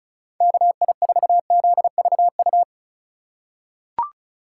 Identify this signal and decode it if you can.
Morse code, heard as a keyed tone, about 34 wpm, 700 Hz. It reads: KI4ZVU
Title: K1USN SST QSO Practice - 35wpm